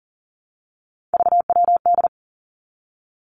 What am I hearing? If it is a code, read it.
Morse code, heard as a keyed tone, about 41 wpm, 700 Hz. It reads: VWD
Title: Three Challenging Letter-Number Combinations 40wpm